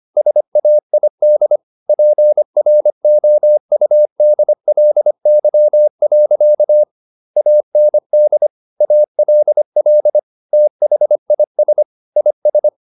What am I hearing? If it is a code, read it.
Morse code, heard as a keyed tone, about 25 wpm, 600 Hz. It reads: SAID PROUDLY. AND ALL THIS IS